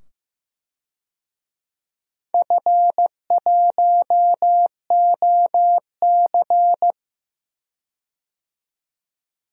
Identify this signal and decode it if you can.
Morse code, heard as a keyed tone, about 15 wpm, 700 Hz. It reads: F1OC
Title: DXCC-100 Call Signs - 4 Characters - 15wpm